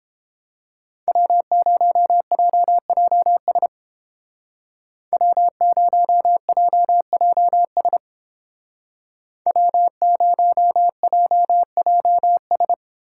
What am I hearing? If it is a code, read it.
Morse code, heard as a keyed tone, about 28 wpm, 700 Hz. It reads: W0JJS W0JJS W0JJS